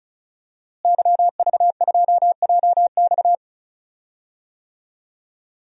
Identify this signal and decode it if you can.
Morse code, heard as a keyed tone, about 34 wpm, 700 Hz. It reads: YV2JX
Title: DXCC-100 Call Signs - 5 Characters - Repeated 3x - 35wpm